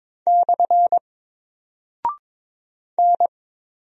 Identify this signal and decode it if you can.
Morse code, heard as a keyed tone, about 22 wpm, 700 Hz. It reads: / N
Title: Single Character (Letter:Number:Punctuation:Prosign) 22wpm